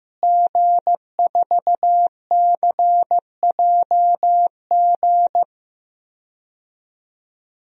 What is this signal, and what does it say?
Morse code, heard as a keyed tone, about 15 wpm, 700 Hz. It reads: G4CJG
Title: DXCC-100 Call Signs - 5 Characters - Repeated 2x - 15wpm